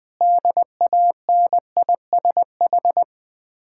Morse code, heard as a keyed tone, about 20 wpm, 700 Hz. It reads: DANISH
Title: Difficult Words with Hints - 20wpm